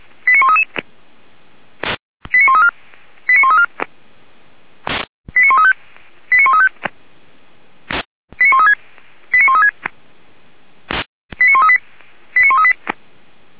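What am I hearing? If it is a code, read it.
Five-tone selective call (ZVEI), heard as quick runs of five tones, tones of about 80 ms. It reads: 89133 89134 89134 89135 89135 89136 89136 89138 89138
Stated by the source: ZVEI five-tone call received on SDR, from the Feuerwehr Orlamünde fire brigade audio server.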